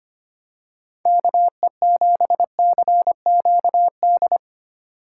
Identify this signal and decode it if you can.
Morse code, heard as a keyed tone, about 25 wpm, 700 Hz. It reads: KE7CQD
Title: USA Call Signs - 6 Characters - 2x Character Spacing - 25wpm